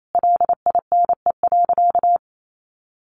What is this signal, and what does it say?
Morse code, heard as a keyed tone, about 28 wpm, 700 Hz. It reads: LINE.
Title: Sentences from Top 1200 Words 28wpm